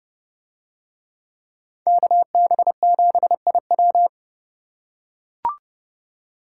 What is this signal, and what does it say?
Morse code, heard as a keyed tone, about 30 wpm, 700 Hz. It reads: KB7IW